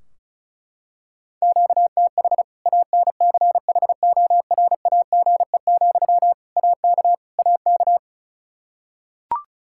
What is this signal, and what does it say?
Morse code, heard as a keyed tone, about 34 wpm, 700 Hz. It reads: QTH ANCHORAGE, AK AK
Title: K1USN SST QSO Practice 35wpm